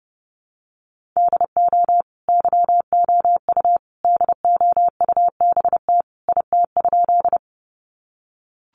Morse code, heard as a keyed tone, about 30 wpm, 700 Hz. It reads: DO YOU DOUBT IT?
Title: Sentences from Top 1500 Words 30wpm